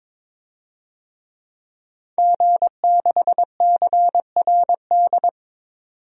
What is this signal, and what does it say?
Morse code, heard as a keyed tone, about 22 wpm, 700 Hz. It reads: G6CRD
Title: DXCC-100 Call Signs - 5 Characters - 22wpm